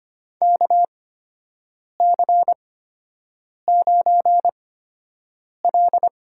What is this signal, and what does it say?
Morse code, heard as a keyed tone, about 25 wpm, spaced as for 7 wpm, 700 Hz. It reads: KC9L
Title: USA Call Signs - 8x Character Spacing - 25wpm